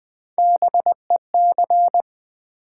Morse code, heard as a keyed tone, about 20 wpm, 700 Hz. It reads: BEC
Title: Build A Word - 3 Letters At A Time - 6-Letter Words - 20wpm